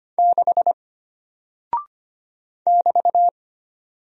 Morse code, heard as a keyed tone, about 25 wpm, 700 Hz. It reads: 6 <BT>
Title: Single Character (Letter:Number:Punctuation:Prosign) Alt 25wpm